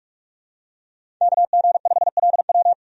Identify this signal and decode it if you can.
Morse code, heard as a keyed tone, about 46 wpm, 700 Hz. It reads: KG5LW
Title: USA Call Signs - 5 Characters - 45wpm